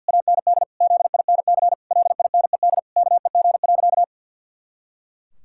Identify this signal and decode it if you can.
Morse code, heard as a keyed tone, about 49 wpm, 700 Hz. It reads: AND ZINC LINED KEG.
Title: Pangram Challenge v2 - 50wpm